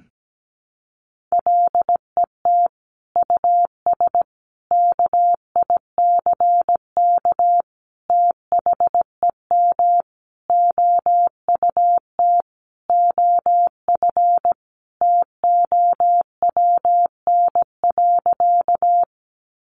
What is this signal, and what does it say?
Morse code, heard as a keyed tone, about 17 wpm, 700 Hz. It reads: LET US KICK THEM OUT OF TOWN.
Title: Sentences from Top 1300 Words 17wpm